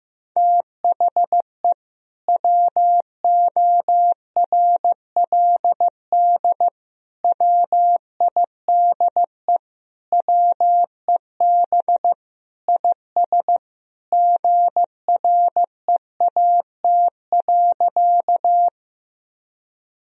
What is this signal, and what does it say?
Morse code, heard as a keyed tone, about 15 wpm, 700 Hz. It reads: THE WORLD WIDE WEB IS GREAT.
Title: Sentences from Top 1000 Words - Review Part 4 - 15wpm